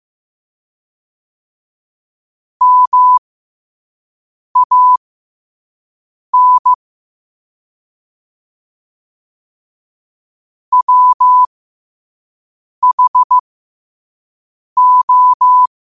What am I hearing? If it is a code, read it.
Morse code, heard as a keyed tone, about 15 wpm, spaced as for 5 wpm, 1000 Hz. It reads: MAN WHO
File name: qotdfiles / 5wpm